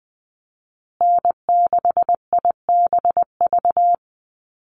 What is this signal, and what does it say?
Morse code, heard as a keyed tone, about 20 wpm, 700 Hz. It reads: N6IBV
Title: USA Call Signs Repeated 2x 20wpm